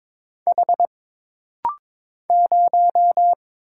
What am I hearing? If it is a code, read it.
Morse code, heard as a keyed tone, about 22 wpm, 700 Hz. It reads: H 0
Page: This rapid-fire timing is geared to push your ICR (Instant Character Recognition) skills.